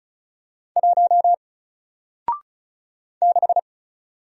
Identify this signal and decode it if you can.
Morse code, heard as a keyed tone, about 35 wpm, 700 Hz. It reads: 1 6
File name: Single-Letter-Number-35wpm